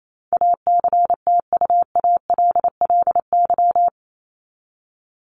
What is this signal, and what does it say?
Morse code, heard as a keyed tone, about 28 wpm, 700 Hz. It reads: ACTUALLY
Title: LY Suffix - Top 100 Words with Base Words 28wpm